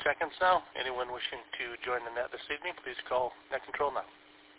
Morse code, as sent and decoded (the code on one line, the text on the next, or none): none